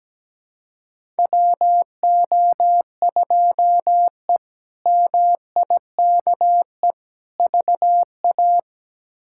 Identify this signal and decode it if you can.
Morse code, heard as a keyed tone, about 17 wpm, 700 Hz. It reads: WO2E MIKE VA
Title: K1USN SST QSO Practice - 17wpm